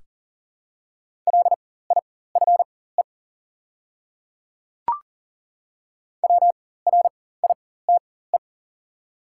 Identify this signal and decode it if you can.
Morse code, heard as a keyed tone, about 40 wpm, spaced as for 19 wpm, 700 Hz. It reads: LIFE WRITE
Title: 4x Character Spacing - Top 500 Words - 40wpm